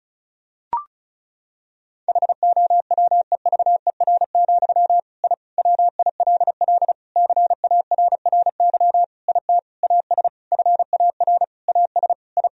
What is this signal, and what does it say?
Morse code, heard as a keyed tone, about 36 wpm, 700 Hz. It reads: HOWEVER, I WILL CARRY IT AS FAR AS I
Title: Sentences from Top 1000 Words - Review Part 1 - 35wpm